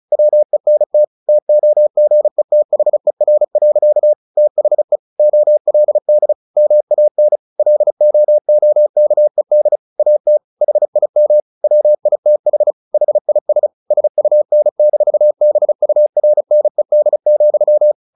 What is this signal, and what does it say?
Morse code, heard as a keyed tone, about 34 wpm, 600 Hz. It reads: WENT TOGETHER. THE OLD MAN LOOKED AT HIM WITH HIS SUN-BURNED,
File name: the old man and the sea / the old man and the sea 1 35wpm